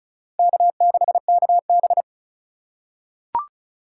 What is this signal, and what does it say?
Morse code, heard as a keyed tone, about 34 wpm, 700 Hz. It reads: K6KB
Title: USA Call Signs - 2x Character Spacing - 35wpm